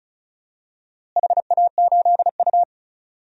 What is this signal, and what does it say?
Morse code, heard as a keyed tone, about 34 wpm, 700 Hz. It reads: HA8U
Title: DXCC-100 Call Signs - 4 Characters - Repeated 2x - 35wpm